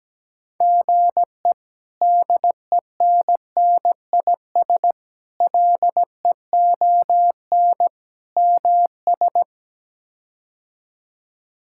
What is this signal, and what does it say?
Morse code, heard as a keyed tone, about 17 wpm, 700 Hz. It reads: GE DENNIS LEON MS
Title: K1USN SST QSO Practice - 17wpm